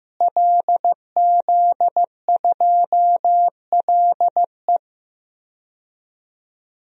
Morse code, heard as a keyed tone, about 15 wpm, 700 Hz. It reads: LZ2LE
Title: DXCC-100 Call Signs - 5 Characters - Repeated 2x - 15wpm